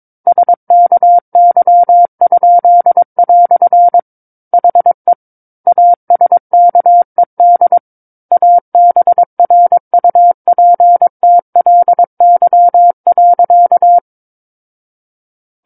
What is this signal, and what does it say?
Morse code, heard as a keyed tone, about 22 wpm, 700 Hz. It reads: SKY?" HE ASKED ABRUPTLY.
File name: Book One / 13-Chapter 13 - 22 WPM